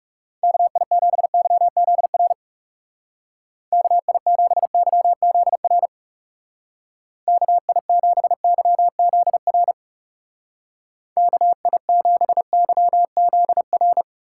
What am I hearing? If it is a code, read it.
Morse code, heard as a keyed tone, about 36 wpm, 700 Hz. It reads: KI7YZR KI7YZR KI7YZR KI7YZR
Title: Speed Racing - USA 2x3 Call Signs 30wpm